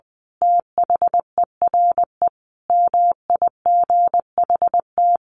Morse code, heard as a keyed tone, about 20 wpm, 700 Hz. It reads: THERE MIGHT
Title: A Sound Of Thunder 20wpm